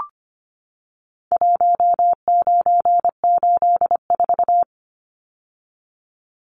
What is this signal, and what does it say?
Morse code, heard as a keyed tone, about 25 wpm, 700 Hz. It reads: 1984
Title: Numbers - Contest Style - 4-Digit 25wpm